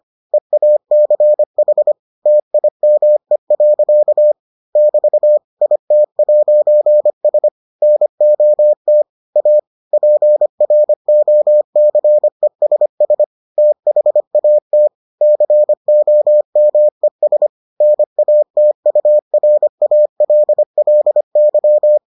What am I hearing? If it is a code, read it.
Morse code, heard as a keyed tone, about 25 wpm, 600 Hz. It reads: EACH TIME. <BT> IT'S NOT A PROCESS THAT COMES NATURALLY